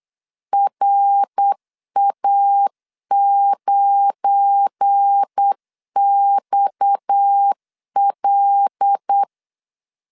Morse code, heard as a keyed tone, about 8 wpm, 800 Hz. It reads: RA9XL
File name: LEZIONI / 7 GF9Z / 7 GF9 8wpm 3545